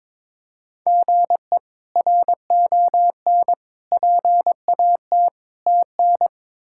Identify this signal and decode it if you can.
Morse code, heard as a keyed tone, about 22 wpm, 700 Hz. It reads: GE RON PAT TN
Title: K1USN SST QSO Practice - 22wpm